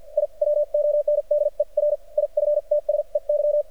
Morse code, heard as a keyed tone, about 50 wpm, spaced as for 40 wpm, 590 Hz. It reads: I WONDER IFTSEY
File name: archived / CRSnet